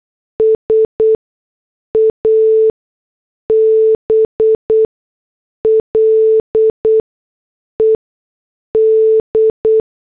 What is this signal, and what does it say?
Morse code, heard as a keyed tone, about 8 wpm, spaced as for 6 wpm, 440 Hz. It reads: SABLED